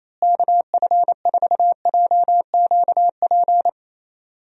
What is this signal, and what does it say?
Morse code, heard as a keyed tone, about 27 wpm, 700 Hz. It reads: KF4JQP
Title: USA Call Signs - 6 Characters - Repeated 2x - 28wpm